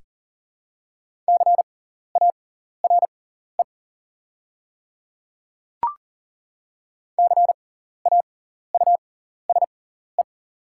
Morse code, heard as a keyed tone, about 37 wpm, spaced as for 14 wpm, 690 Hz. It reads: CARE CAUSE
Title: 6x Character Spacing - Top 300 Words - 40wpm